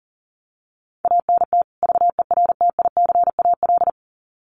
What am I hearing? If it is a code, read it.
Morse code, heard as a keyed tone, about 40 wpm, 700 Hz. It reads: ANT VERTICAL
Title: Common QSO Phrases 40wpm